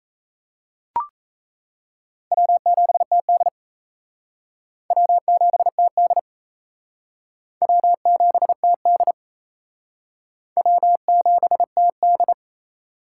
Morse code, heard as a keyed tone, about 36 wpm, 700 Hz. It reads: W7TD W7TD W7TD W7TD